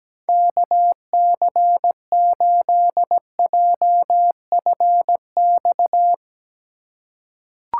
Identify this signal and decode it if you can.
Morse code, heard as a keyed tone, about 17 wpm, 700 Hz. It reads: KC8JFX
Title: USA Call Signs - 6 Characters - 4x Character Spacing - 17wpm